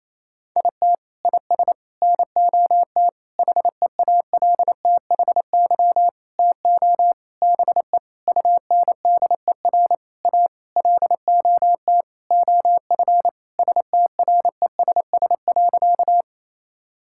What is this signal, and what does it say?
Morse code, heard as a keyed tone, about 28 wpm, 700 Hz. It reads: IT IS NOT HEALTHY TO BE UNDER A LOT OF STRESS.
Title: Sentences from Top 1000 Words - Review Part 6 - 28wpm